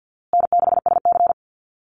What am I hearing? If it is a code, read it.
Morse code, heard as a keyed tone, about 50 wpm, 700 Hz. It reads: N6SC
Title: USA Call Signs Repeated 3x 50wpm